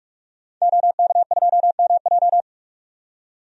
Morse code, heard as a keyed tone, about 42 wpm, 700 Hz. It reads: OK2MJ
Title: DXCC-100 Call Signs - 5 Characters - 45wpm